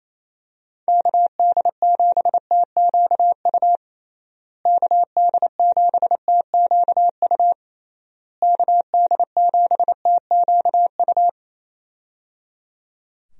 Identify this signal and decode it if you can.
Morse code, heard as a keyed tone, about 28 wpm, 700 Hz. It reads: KD7TQU KD7TQU KD7TQU
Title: USA Call Signs - 6 Characters - Repeated 3x - 28wpm